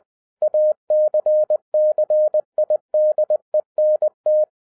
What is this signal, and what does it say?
Morse code, heard as a keyed tone, about 20 wpm, 610 Hz. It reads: ACCIDENT
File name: 2011-10-mp3 / n03-TARO JR0QWW 20wpm